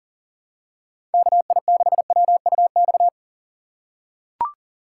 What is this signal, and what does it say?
Morse code, heard as a keyed tone, about 37 wpm, 700 Hz. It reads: KI6WUX
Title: USA Call Signs - 6 Characters - 6x Character Spacing - 40wpm